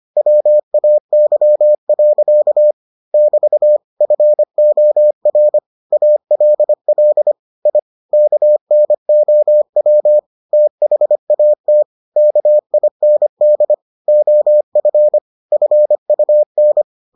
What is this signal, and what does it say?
Morse code, heard as a keyed tone, about 25 wpm, 600 Hz. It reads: WAY. <BT> FOR ALL I KNOW THAT KIND OF FUN